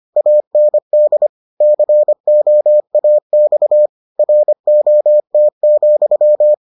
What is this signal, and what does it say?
Morse code, heard as a keyed tone, about 25 wpm, 600 Hz. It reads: AND COAX ROT,